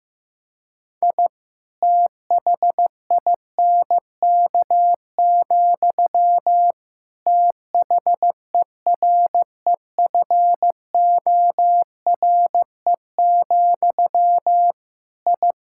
Morse code, heard as a keyed tone, about 15 wpm, 700 Hz. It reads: I THINK, THEREFORE, I
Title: Sentences from Top 1000 Words - Review Part 5 - 15wpm